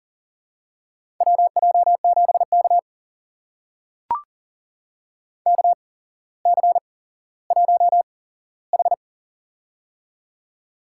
Morse code, heard as a keyed tone, about 40 wpm, spaced as for 17 wpm, 700 Hz. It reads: WJ7K KC1H
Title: USA Call Signs - 4 Characters - 8x Character Spacing - 40wpm